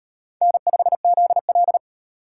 Morse code, heard as a keyed tone, about 36 wpm, 700 Hz. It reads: N5ZL